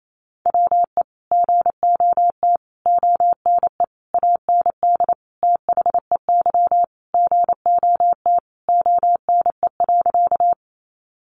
Morse code, heard as a keyed tone, about 28 wpm, 700 Hz. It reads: WE GOT ONE AND THEY GOT ONE.
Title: Sentences from Top 300 Words 28wpm